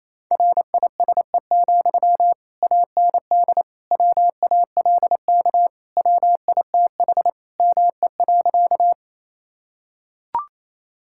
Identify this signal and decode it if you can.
Morse code, heard as a keyed tone, about 28 wpm, 700 Hz. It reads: RISE, AND WALK WITH ME.
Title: Sentences from Top 1000 Words - Review Part 6 - 28wpm